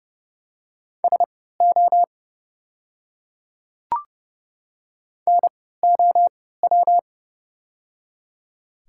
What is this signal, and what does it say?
Morse code, heard as a keyed tone, about 30 wpm, spaced as for 17 wpm, 700 Hz. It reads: SO NOW
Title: 3x Character Spacing - Top 100 Words - 30wpm